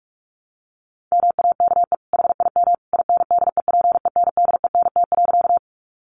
Morse code, heard as a keyed tone, about 44 wpm, 700 Hz. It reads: MAKE HIM INDEPENDENT.
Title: Sentences from Top 900 Words 45wpm